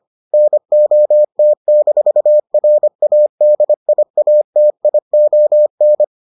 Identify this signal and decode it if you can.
Morse code, heard as a keyed tone, about 25 wpm, 600 Hz. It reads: NOT-RADIATION